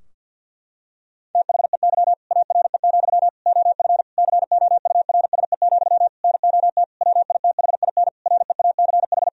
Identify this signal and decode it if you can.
Morse code, heard as a keyed tone, about 50 wpm, 700 Hz. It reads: THEY ARE, OF COURSE, NOT WITHIN REACH
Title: Sentences from Top 1000 Words - Review Part 4 - 50wpm